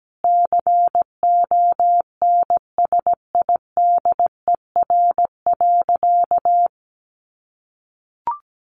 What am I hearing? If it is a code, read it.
Morse code, heard as a keyed tone, about 17 wpm, 700 Hz. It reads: CONSIDER.
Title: Sentences from Top 900 Words 17wpm